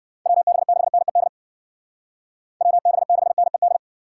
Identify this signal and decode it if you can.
Morse code, heard as a keyed tone, about 60 wpm, 700 Hz. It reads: WB6NED WB6NED
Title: Speed Racing - USA 2x3 Call Signs 50wpm